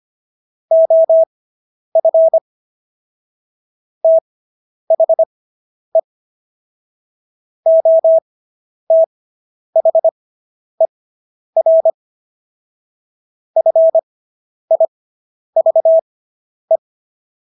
Morse code, heard as a keyed tone, about 25 wpm, spaced as for 10 wpm, 650 Hz. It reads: OF THE OTHER FIVE